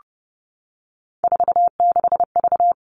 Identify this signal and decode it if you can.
Morse code, heard as a keyed tone, about 28 wpm, 700 Hz. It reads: 46V
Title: All Cut Numbers - 3-Digit 30wpm